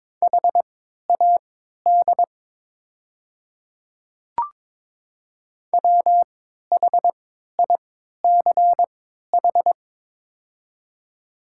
Morse code, heard as a keyed tone, about 22 wpm, spaced as for 12 wpm, 700 Hz. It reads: HAD WHICH
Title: 3x Character Spacing - Top 100 Words - 22wpm